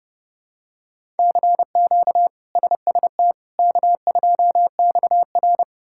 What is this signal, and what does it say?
Morse code, heard as a keyed tone, about 29 wpm, 700 Hz. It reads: CQ SST K2XR
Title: K1USN SST QSO Practice - 30wpm